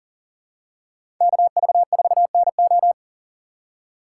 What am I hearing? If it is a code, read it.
Morse code, heard as a keyed tone, about 40 wpm, 690 Hz. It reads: KV4NO
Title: USA Call Signs - 5 Characters - 3x Character Spacing - 40wpm